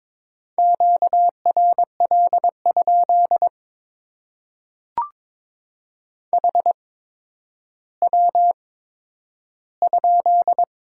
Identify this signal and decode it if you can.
Morse code, heard as a keyed tone, about 22 wpm, spaced as for 11 wpm, 700 Hz. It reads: QRL? HW?
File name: Top 90 QSO Elements - 8x Character Spacing - 22wpm